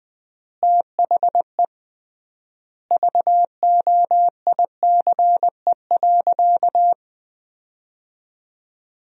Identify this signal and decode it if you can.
Morse code, heard as a keyed tone, about 20 wpm, 700 Hz. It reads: THE VOICE.
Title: 3x Word-Spacing - Sentences from Top 500 Words - 20wpm